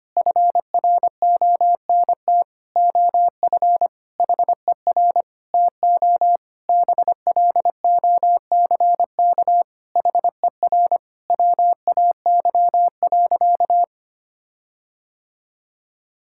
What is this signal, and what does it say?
Morse code, heard as a keyed tone, about 25 wpm, 700 Hz. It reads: FRONT OF HER TO BLOCK HER WAY.
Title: Sentences from Top 1600 Words 25wpm